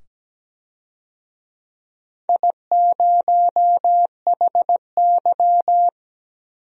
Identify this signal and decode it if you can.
Morse code, heard as a keyed tone, about 17 wpm, 700 Hz. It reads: I0HY
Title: DXCC-100 Call Signs - 4 Characters - 17wpm